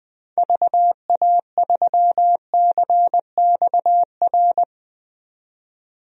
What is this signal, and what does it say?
Morse code, heard as a keyed tone, about 21 wpm, 700 Hz. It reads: VA3CXR
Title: DXCC-100 Call Signs - 6 Characters - 20wpm